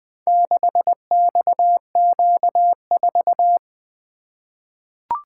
Morse code, heard as a keyed tone, about 20 wpm, 700 Hz. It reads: 6XQ4
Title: Four Letter-Number Combinations - 20wpm